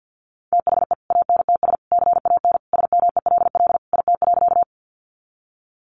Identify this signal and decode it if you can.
Morse code, heard as a keyed tone, about 50 wpm, 700 Hz. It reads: THE ANTS CAN SMELL IT.
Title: Easy Phrases 50wpm